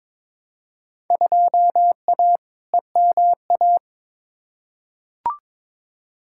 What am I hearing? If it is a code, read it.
Morse code, heard as a keyed tone, about 22 wpm, 700 Hz. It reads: 2A EMA
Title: Field Day Exchanges - 22wpm